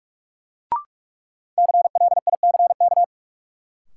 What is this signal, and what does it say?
Morse code, heard as a keyed tone, about 46 wpm, 700 Hz. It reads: CLICK
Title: Sentences from Top 1100 Words - New Word Drill - 45wpm